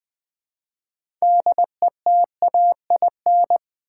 Morse code, heard as a keyed tone, about 20 wpm, 700 Hz. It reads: DETAIN
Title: 6 Letter Words - First 3 Letters Given - 20wpm